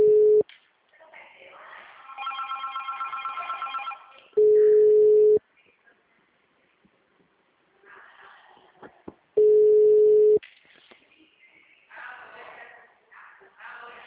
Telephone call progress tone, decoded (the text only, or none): ringback tone